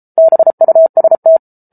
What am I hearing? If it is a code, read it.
Morse code, heard as a keyed tone, about 34 wpm, 650 Hz. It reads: BUST